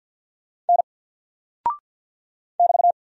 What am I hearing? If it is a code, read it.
Morse code, heard as a keyed tone, about 50 wpm, 700 Hz. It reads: N <BT>
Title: Single Character (Letter:Number:Punctuation:Prosign) 50wpm